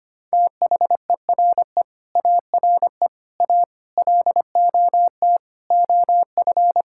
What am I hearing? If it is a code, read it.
Morse code, heard as a keyed tone, about 25 wpm, 700 Hz. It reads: THERE ARE A LOT OF
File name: practice / 25wpm / Sentences with 9-Word Phrases - 25wpm